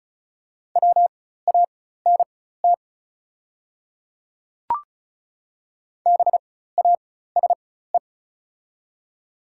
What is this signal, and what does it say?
Morse code, heard as a keyed tone, about 36 wpm, spaced as for 16 wpm, 690 Hz. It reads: WANT BASE
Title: 4x Character Spacing - Top 300 Words - 35wpm